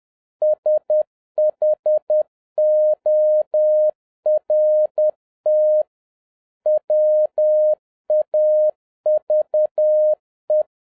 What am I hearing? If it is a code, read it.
Morse code, heard as a keyed tone, about 10 wpm, 610 Hz. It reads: SHORT WAVE
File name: 2011-10-mp3 / n11-MEETING WITH GOOD FRIENDS 10wpm